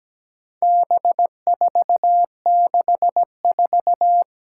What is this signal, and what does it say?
Morse code, heard as a keyed tone, about 17 wpm, 700 Hz. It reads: B464
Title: Four Challenging Letter-Number Combinations 17wpm